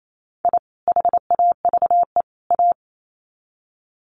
Morse code, heard as a keyed tone, about 28 wpm, 700 Hz. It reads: I HAVE A
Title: Sets of 3 Words 28wpm